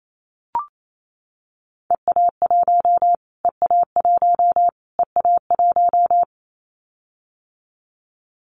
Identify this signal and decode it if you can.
Morse code, heard as a keyed tone, about 28 wpm, 700 Hz. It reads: EA1 EA1 EA1
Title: Callsign Prefixes for Top 25 DXCC - Repeated 3x - 28wpm